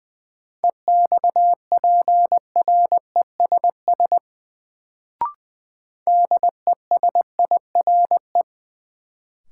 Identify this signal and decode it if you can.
Morse code, heard as a keyed tone, about 20 wpm, 700 Hz. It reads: EXPRESS DESIRE
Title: Sentences from Top 1100 Words - New Word Drill Alt - 20wpm